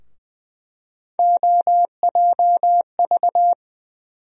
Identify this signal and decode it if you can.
Morse code, heard as a keyed tone, about 20 wpm, 700 Hz. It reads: OJV
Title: USA Call Sign Postfixes - 3 Letters 20wpm